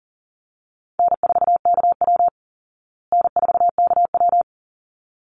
Morse code, heard as a keyed tone, about 40 wpm, 700 Hz. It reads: N4KW N4KW
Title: Easy Call Signs Repeated 2x - 40wpm